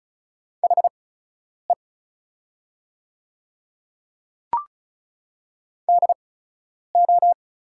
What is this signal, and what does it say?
Morse code, heard as a keyed tone, about 36 wpm, spaced as for 10 wpm, 690 Hz. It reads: HE DO
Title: 8x Character Spacing - Top 100 Words - 35wpm